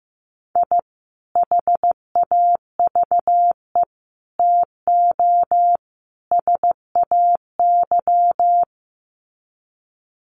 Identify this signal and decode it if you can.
Morse code, heard as a keyed tone, about 15 wpm, 700 Hz. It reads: I HAVE TO SAY
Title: Sets of 4 Words 15wpm